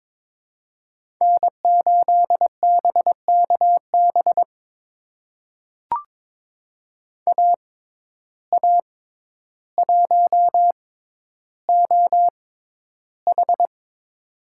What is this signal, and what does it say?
Morse code, heard as a keyed tone, about 23 wpm, spaced as for 11 wpm, 700 Hz. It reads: N8BKB AA1OH
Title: USA Call Signs - 5 Characters - 6x Character Spacing - 22wpm